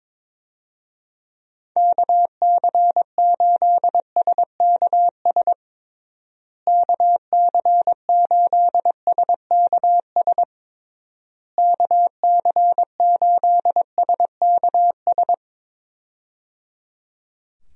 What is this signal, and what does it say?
Morse code, heard as a keyed tone, about 22 wpm, 700 Hz. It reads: KC8SKS KC8SKS KC8SKS
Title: USA Call Signs - 6 Characters - Repeated 3x - 22wpm